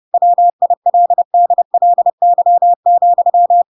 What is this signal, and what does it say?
Morse code, heard as a keyed tone, about 30 wpm, 700 Hz. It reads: WILDLY,
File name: Pinocchio / Pinocchio ch6 30 30 ik1hge